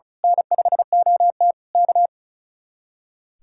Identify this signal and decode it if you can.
Morse code, heard as a keyed tone, about 34 wpm, 700 Hz. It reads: N5OT K
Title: Speed-Racing Common QSO Phrases - 35wpm